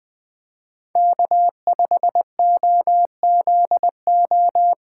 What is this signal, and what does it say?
Morse code, heard as a keyed tone, about 19 wpm, 700 Hz. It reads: K5OZO